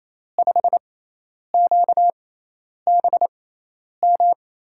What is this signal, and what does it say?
Morse code, heard as a keyed tone, about 28 wpm, spaced as for 10 wpm, 700 Hz. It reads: 5QBM
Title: USA Call Signs - 5 Characters - 6x Character Spacing - 28wpm